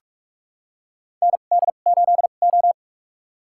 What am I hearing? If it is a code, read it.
Morse code, heard as a keyed tone, about 41 wpm, spaced as for 31 wpm, 700 Hz. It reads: ND8O
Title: USA Call Signs - 4 Characters - 2x Character Spacing - 45wpm